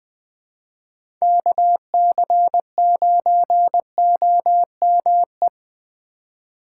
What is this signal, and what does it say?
Morse code, heard as a keyed tone, about 20 wpm, 700 Hz. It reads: KC9OME
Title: USA Call Signs - 6 Characters - 2x Character Spacing - 20wpm